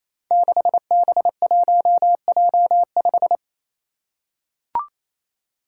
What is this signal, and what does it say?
Morse code, heard as a keyed tone, about 27 wpm, 700 Hz. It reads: 6B1J5
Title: Five Challenging Letter-Number Combinations 28wpm